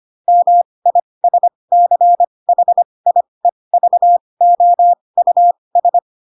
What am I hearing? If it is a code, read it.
Morse code, heard as a keyed tone, about 25 wpm, spaced as for 20 wpm, 700 Hz. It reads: MISCHIEVOUS
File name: Pinocchio / Pinocchio ch8 15 25 ik1hge